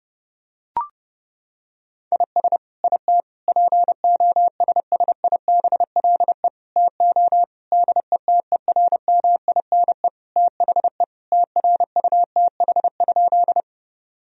Morse code, heard as a keyed tone, about 30 wpm, 700 Hz. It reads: IS IT POSSIBLE TO DETERMINE THE TRUTH?
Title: Sentences from Top 1000 Words - Review Part 4 - 30wpm